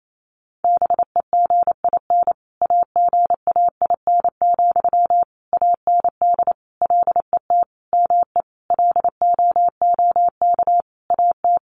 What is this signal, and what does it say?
Morse code, heard as a keyed tone, about 28 wpm, 700 Hz. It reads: BEGIN AGAIN, AND LET ME LOOK AT
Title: Sentences from Top 800 Words 28wpm